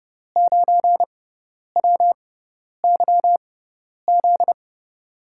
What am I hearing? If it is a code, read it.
Morse code, heard as a keyed tone, about 32 wpm, spaced as for 10 wpm, 700 Hz. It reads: 9WYZ
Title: USA Call Signs - 6 Characters - 6x Character Spacing - 30wpm